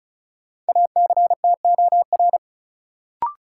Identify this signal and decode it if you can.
Morse code, heard as a keyed tone, about 34 wpm, 700 Hz. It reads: ACTOR
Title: Sentences from Top 1500 Words - New Word Drill Alt - 35wpm